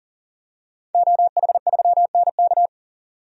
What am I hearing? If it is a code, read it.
Morse code, heard as a keyed tone, about 40 wpm, 700 Hz. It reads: OH3NK
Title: DXCC-100 Call Signs - 5 Characters - 40wpm